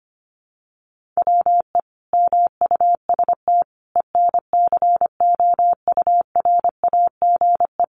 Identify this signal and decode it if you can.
Morse code, heard as a keyed tone, about 25 wpm, 700 Hz. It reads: WE MUST ENCOURAGE
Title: Sentences from Top 1400 Words 25wpm